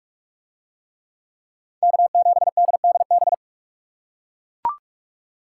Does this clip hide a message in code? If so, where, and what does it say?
Yes: Morse code from 1.8 to 3.3 s, heard as a keyed tone, about 41 wpm, 700 Hz. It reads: K7DDB